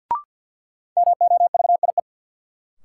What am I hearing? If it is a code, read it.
Morse code, heard as a keyed tone, about 51 wpm, 700 Hz. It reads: MOVIE
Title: Sentences from Top 1000 Words - Review Part 5 - New Word Drill - 50wpm